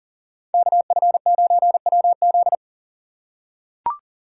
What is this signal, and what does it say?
Morse code, heard as a keyed tone, about 40 wpm, 700 Hz. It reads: KF9WZ
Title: USA Call Signs - 3x Character Spacing - 40wpm